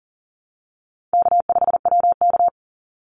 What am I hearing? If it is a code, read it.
Morse code, heard as a keyed tone, about 40 wpm, 700 Hz. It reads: K5WK
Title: Easy Call Signs Repeated 2x - 40wpm